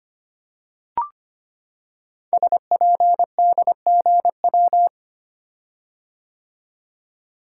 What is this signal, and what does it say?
Morse code, heard as a keyed tone, about 25 wpm, 700 Hz. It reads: SPDGW
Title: Code Groups - 5 Letters - 25wpm